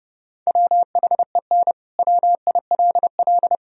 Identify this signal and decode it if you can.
Morse code, heard as a keyed tone, about 32 wpm, 700 Hz. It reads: WHEN WILL
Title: Sentences from Top 1000 Words - Review Part 2 - 30wpm